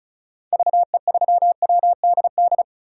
Morse code, heard as a keyed tone, about 34 wpm, 700 Hz. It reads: VE3WDD
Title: DXCC-100 Call Signs - 6 Characters - 35wpm